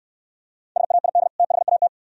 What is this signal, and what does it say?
Morse code, heard as a keyed tone, about 67 wpm, 700 Hz. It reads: SAID THAT